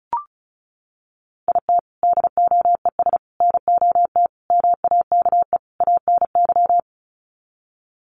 Morse code, heard as a keyed tone, about 34 wpm, 700 Hz. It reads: IT DOES NOT MAKE ANY
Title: Sets of 5 Words Encore2 35wpm